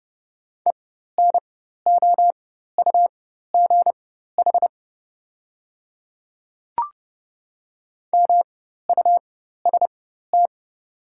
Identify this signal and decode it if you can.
Morse code, heard as a keyed tone, about 30 wpm, spaced as for 14 wpm, 700 Hz. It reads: ENOUGH MUST
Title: 4x Character Spacing - Top 500 Words - 30wpm